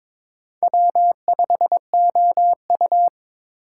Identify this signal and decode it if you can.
Morse code, heard as a keyed tone, about 22 wpm, 700 Hz. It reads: W5OU
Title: USA Call Signs - 4x Character Spacing - 22wpm